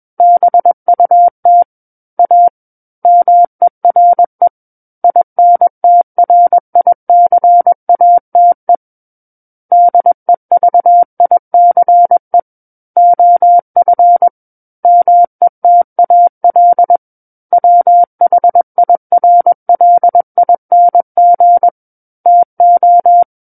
Morse code, heard as a keyed tone, about 21 wpm, 700 Hz. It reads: BUT A MERE INTRICATE DEVICE OF METAL WHIRLING TO
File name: Book One / 12-Chapter 12 - 21 WPM